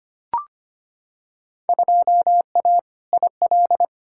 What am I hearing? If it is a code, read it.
Morse code, heard as a keyed tone, about 25 wpm, 700 Hz. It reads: 2A IL
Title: Field Day Exchanges - 25wpm